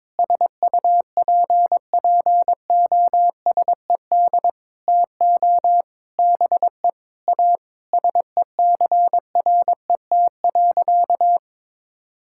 Morse code, heard as a keyed tone, about 22 wpm, 700 Hz. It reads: SUPPOSED TO BE A SECRET.
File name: practice / 22wpm / Sentences with 4-Word Phrases - 22wpm